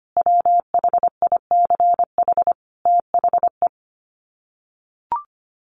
Morse code, heard as a keyed tone, about 25 wpm, 700 Hz. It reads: WHICH THE
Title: Sets of 5 Words Encore2 25wpm